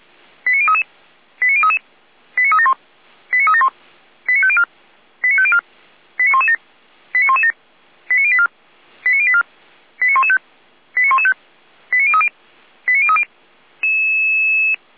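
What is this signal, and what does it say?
Five-tone selective call (ZVEI), heard as quick runs of five tones, tones of about 70 ms. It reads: 89033 89033 89371 89371 89504 89504 89117 89117 89084 89084 89115 89115 89030 89030
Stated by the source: ZVEI five-tone selective call received by SDR.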